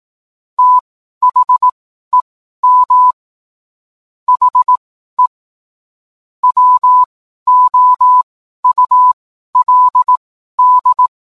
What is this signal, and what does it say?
Morse code, heard as a keyed tone, about 18 wpm, spaced as for 12 wpm, 1000 Hz. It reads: THEM HE WOULD